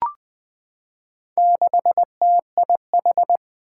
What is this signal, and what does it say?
Morse code, heard as a keyed tone, about 20 wpm, 700 Hz. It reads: 6TIH
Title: Four Challenging Letter-Number Combinations 20wpm